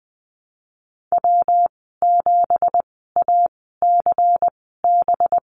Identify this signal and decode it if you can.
Morse code, heard as a keyed tone, about 20 wpm, spaced as for 15 wpm, 700 Hz. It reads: W7ACB
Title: 186 Lesson 30g - Letter b - Call Signs 20wpm